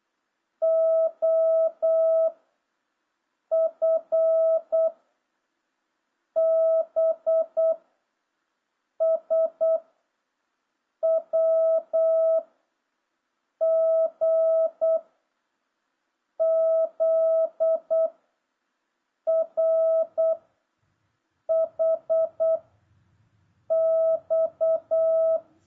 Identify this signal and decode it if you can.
Morse code, heard as a keyed tone, about 8 wpm, 650 Hz. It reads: O F B S W G Z R H X